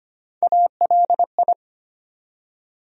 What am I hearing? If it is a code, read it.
Morse code, heard as a keyed tone, about 25 wpm, 700 Hz. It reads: ALI
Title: USA Call Sign Postfixes - 3 Letters 25wpm